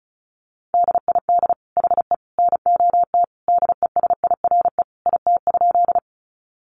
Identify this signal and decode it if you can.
Morse code, heard as a keyed tone, about 34 wpm, 700 Hz. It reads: DID HE NOT DESIRE IT?
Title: Sentences from Top 1100 Words 35wpm